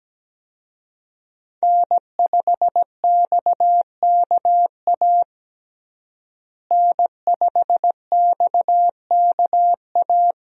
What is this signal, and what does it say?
Morse code, heard as a keyed tone, about 17 wpm, 700 Hz. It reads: N5XKA N5XKA
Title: USA Call Signs - 5 Characters - Repeated 2x - 17wpm